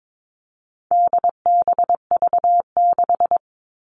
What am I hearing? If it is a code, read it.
Morse code, heard as a keyed tone, about 22 wpm, 700 Hz. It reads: DBV6
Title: All Cut Numbers - 4-Digit 22wpm